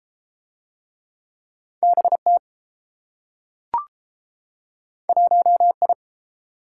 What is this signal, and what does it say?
Morse code, heard as a keyed tone, about 29 wpm, 700 Hz. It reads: BT 1I